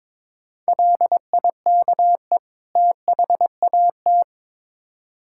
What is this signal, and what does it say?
Morse code, heard as a keyed tone, about 22 wpm, 700 Hz. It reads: LIKE THAT
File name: Build A Phrase - 2-Word Phrases - 22wpm